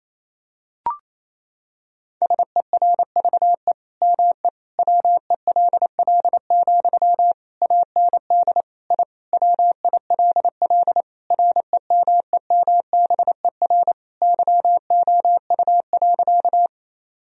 Morse code, heard as a keyed tone, about 28 wpm, 700 Hz. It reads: SERVE ME WELL, AND I WILL REMEMBER YOU.
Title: Sentences from Top 1000 Words - Review Part 2 - 28wpm